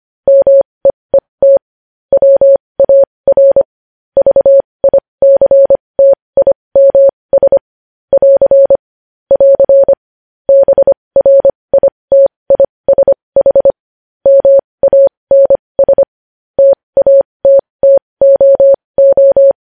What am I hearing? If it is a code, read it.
Morse code, heard as a keyed tone, about 25 wpm, spaced as for 20 wpm, 550 Hz. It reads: MEET WAR VICTIMS <AR> <AR> BRITISH MANS TATTOO
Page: News Headlines in Morse Code at 20 WPM